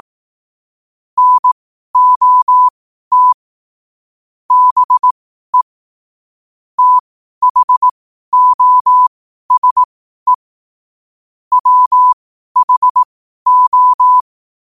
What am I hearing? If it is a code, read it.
Morse code, heard as a keyed tone, about 18 wpm, spaced as for 12 wpm, 1000 Hz. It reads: NOT BE THOSE WHO
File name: qotdfiles / 12wpm